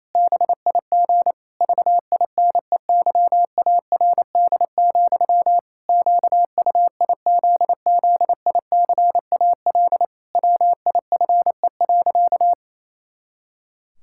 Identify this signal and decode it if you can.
Morse code, heard as a keyed tone, about 28 wpm, 700 Hz. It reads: BIG VINEYARD, QUIZZICAL WIFE.
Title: Pangram Challenge v2 - 28wpm